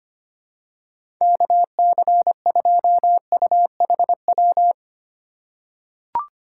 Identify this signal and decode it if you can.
Morse code, heard as a keyed tone, about 25 wpm, 700 Hz. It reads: KC2UHW
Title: USA Call Signs - 8x Character Spacing - 25wpm